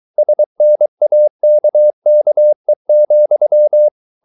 Morse code, heard as a keyed tone, about 23 wpm, 600 Hz. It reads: SNAKKE,
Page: Ferie 23 wpm